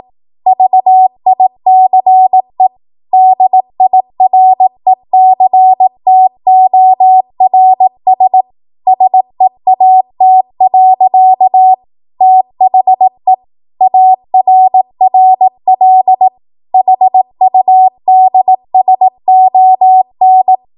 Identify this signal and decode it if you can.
Morse code, heard as a keyed tone, about 18 wpm, 750 Hz. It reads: VICE DIRECTORS SEAT. THE ARRL HUDSON
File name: Archive / 18 WPM / 18WPM